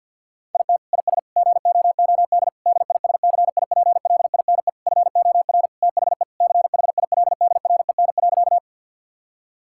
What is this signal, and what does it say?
Morse code, heard as a keyed tone, about 50 wpm, 700 Hz. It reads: IT IS GOOD DISCIPLINE FOR THE CHILDREN.
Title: Sentences from Top 1000 Words - Review Part 6 - 50wpm